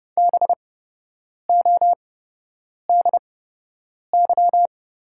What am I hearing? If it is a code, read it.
Morse code, heard as a keyed tone, about 30 wpm, spaced as for 8 wpm, 700 Hz. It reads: BODY
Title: 8x Character Spacing - Top 500 Words - 30wpm